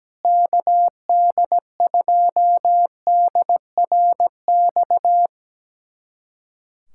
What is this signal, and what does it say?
Morse code, heard as a keyed tone, about 17 wpm, 700 Hz. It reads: KD2DRX
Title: USA Callsign DE Callsign - 6 Characters - 17wpm